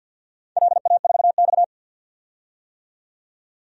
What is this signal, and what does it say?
Morse code, heard as a keyed tone, about 52 wpm, 700 Hz. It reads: LA4X
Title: DXCC-100 Call Signs - 4 Characters - 50wpm